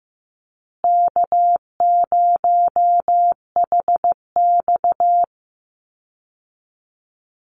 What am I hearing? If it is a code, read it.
Morse code, heard as a keyed tone, about 15 wpm, 700 Hz. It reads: K0HX
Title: Easy Call Signs Repeated 3x - 15wpm